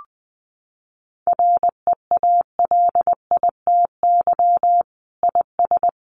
Morse code, heard as a keyed tone, about 20 wpm, 700 Hz. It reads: REALITY IS
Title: Sentences from Top 1200 Words 20wpm